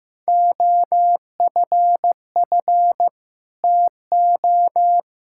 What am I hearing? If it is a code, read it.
Morse code, heard as a keyed tone, about 15 wpm, 700 Hz. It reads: OFF TO
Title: Sentences from Top 1600 Words 15wpm